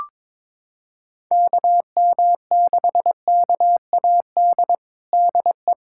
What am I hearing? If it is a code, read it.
Morse code, heard as a keyed tone, about 22 wpm, 700 Hz. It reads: KM6KAD DE
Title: USA Callsign DE Callsign - 6 Characters - 22wpm